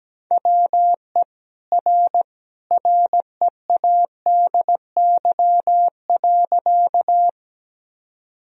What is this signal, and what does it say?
Morse code, heard as a keyed tone, about 17 wpm, 700 Hz. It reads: WE R READY.
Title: Sentences from Top 400 Words and Abbreviations 17wpm